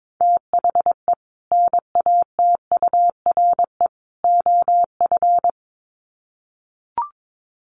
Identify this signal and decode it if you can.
Morse code, heard as a keyed tone, about 22 wpm, 700 Hz. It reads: THE NATURE OF
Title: Sets of 3 Words 22wpm